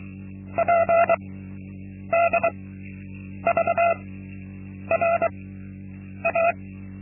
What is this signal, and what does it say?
Morse code, heard as a keyed tone, about 23 wpm, spaced as for 8 wpm, 680 Hz. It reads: PDVRA
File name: Linksmakalnis-2